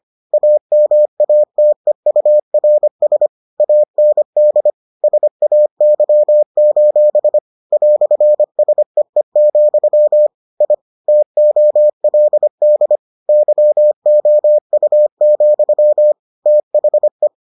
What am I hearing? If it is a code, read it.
Morse code, heard as a keyed tone, about 25 wpm, 600 Hz. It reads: AMATEURS AND SAY: "SEE, I TOLD YOU, THE